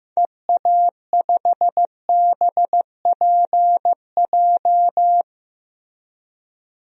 Morse code, heard as a keyed tone, about 15 wpm, 700 Hz. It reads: EA5BPJ
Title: DXCC-100 Call Signs - 6 Characters - Repeated 3x - 15wpm